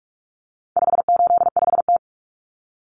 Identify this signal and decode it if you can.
Morse code, heard as a keyed tone, about 44 wpm, 700 Hz. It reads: 585T
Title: Common Cut Numbers - 4-Digit 45wpm